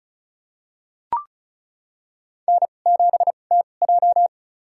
Morse code, heard as a keyed tone, about 36 wpm, spaced as for 25 wpm, 700 Hz. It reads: N7TJ
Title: USA Call Signs - 4 Characters - 2x Character Spacing - 35wpm